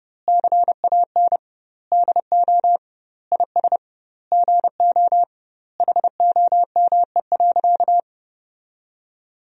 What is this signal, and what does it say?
Morse code, heard as a keyed tone, about 30 wpm, 700 Hz. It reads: CAN DO IS GO HOME.
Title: 2x Word-Spacing - Sentences from Top 200 Words - 30wpm